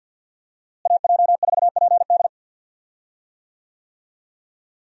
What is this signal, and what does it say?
Morse code, heard as a keyed tone, about 52 wpm, 700 Hz. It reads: AJ4PD
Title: USA Call Signs - 5 Characters - Repeated 2x - 50wpm